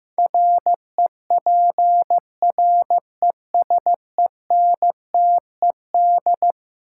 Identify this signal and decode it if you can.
Morse code, heard as a keyed tone, about 15 wpm, 700 Hz. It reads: REPRESENTED
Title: Sets of 5 Words - New Word Drill - 15wpm